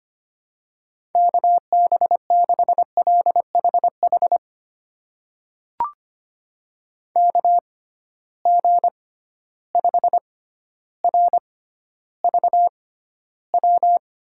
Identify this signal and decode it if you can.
Morse code, heard as a keyed tone, about 25 wpm, spaced as for 13 wpm, 700 Hz. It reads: KB6LHH KG5RVW
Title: USA Call Signs - 6 Characters - 6x Character Spacing - 25wpm